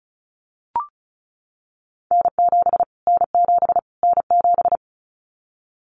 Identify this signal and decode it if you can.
Morse code, heard as a keyed tone, about 34 wpm, 700 Hz. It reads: N7 N7 N7
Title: Callsign Prefixes for Top 25 DXCC - Repeated 3x - 35wpm